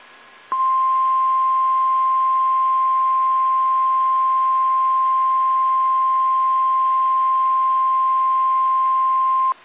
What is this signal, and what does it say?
Emergency Alert System (SAME) tones (the 1050 Hz attention tone). It is heard from 1.0 to 8.5 s.
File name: eas / KSDZKDJL / KSDZKDJL